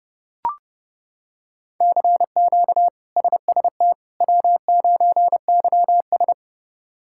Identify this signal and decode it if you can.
Morse code, heard as a keyed tone, about 30 wpm, 700 Hz. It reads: CQ SST W9YS
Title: K1USN SST QSO Practice - 30wpm